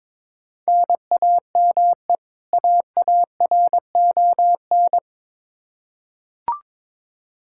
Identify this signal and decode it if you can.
Morse code, heard as a keyed tone, about 22 wpm, 700 Hz. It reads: NAME AARON
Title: Practice QSO - Exchange 1 - Beginning - 22wpm